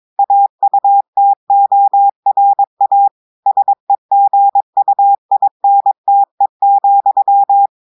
Morse code, heard as a keyed tone, about 22 wpm, 800 Hz. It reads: AUTORA SEGUINTE,
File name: audios / portugais / portugais texte en clair 1 vitesse 22 mots minutes